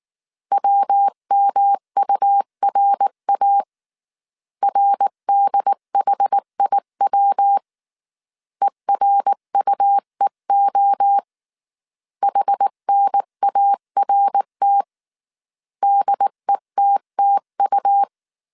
Morse code, heard as a keyed tone, about 19 wpm, 800 Hz. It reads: WMURA RDHIW ERUEO HNART DETTU
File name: CW / DET / 1 3 1500C S B 18wpm 1916